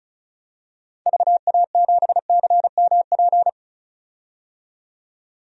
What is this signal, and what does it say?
Morse code, heard as a keyed tone, about 34 wpm, 700 Hz. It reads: VA7CMP
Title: DXCC-100 Call Signs - 6 Characters - 35wpm